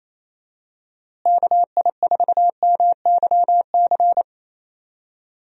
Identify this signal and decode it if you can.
Morse code, heard as a keyed tone, about 28 wpm, 700 Hz. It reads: KI4MYC
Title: USA Call Signs - 6 Characters - Repeated 3x - 28wpm